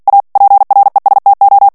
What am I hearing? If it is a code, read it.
Morse code, heard as a keyed tone, about 48 wpm, 800 Hz. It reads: A PRESTO
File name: morse mp3 / ITALIEN texte clair 1 vitesse 45 mots minute